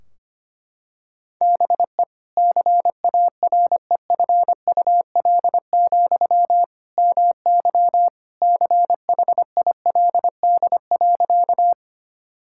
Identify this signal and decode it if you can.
Morse code, heard as a keyed tone, about 25 wpm, 700 Hz. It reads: BE CAREFUL, MY CHILD.
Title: Sentences from Top 1600 Words 25wpm